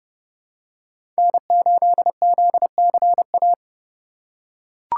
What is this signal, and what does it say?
Morse code, heard as a keyed tone, about 30 wpm, 700 Hz. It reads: N8ZCA
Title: USA Call Signs - 5 Characters - 4x Character Spacing - 30wpm